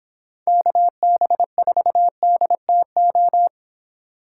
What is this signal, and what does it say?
Morse code, heard as a keyed tone, about 26 wpm, 700 Hz. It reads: KB4DTO